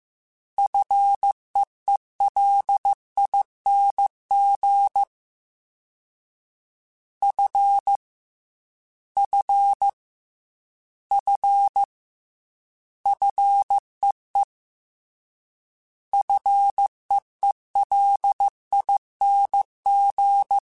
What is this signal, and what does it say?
Morse code, heard as a keyed tone, about 15 wpm, 800 Hz. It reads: FEELING F F F FEE FEELING
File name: Fs / Single F mp3s / Single F Prefixes 3x2 Twice @13